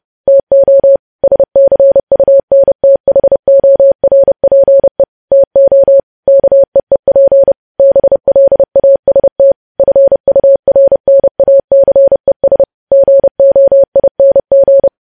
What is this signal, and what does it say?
Morse code, heard as a keyed tone, about 30 wpm, 550 Hz. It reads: TO SCUNTHORPE TO KEEP BLAST FURNACES GOING
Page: news headlines in morse code at 30 wpm